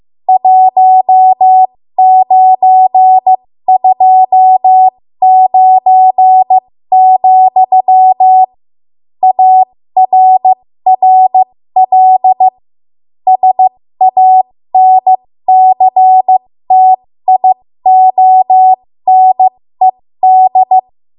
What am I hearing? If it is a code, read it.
Morse code, heard as a keyed tone, about 15 wpm, spaced as for 13 wpm, 750 Hz. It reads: 1929, ARRL SANCTIONED
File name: Morse / 13WPM